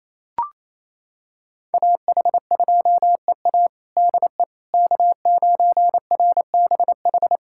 Morse code, heard as a keyed tone, about 28 wpm, 700 Hz. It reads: AH2EA DE K9RBH
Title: USA Callsign DE Callsign - 5 Characters - 28wpm